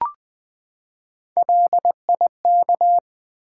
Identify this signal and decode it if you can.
Morse code, heard as a keyed tone, about 20 wpm, 690 Hz. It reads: LIK
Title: Build A Word - 3 Letters At A Time - 8-Letter Words - 20wpm